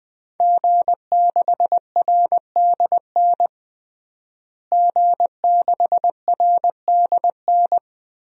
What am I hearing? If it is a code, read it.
Morse code, heard as a keyed tone, about 20 wpm, 700 Hz. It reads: G6RDN G6RDN
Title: DXCC-100 Call Signs - 5 Characters - Repeated 2x - 20wpm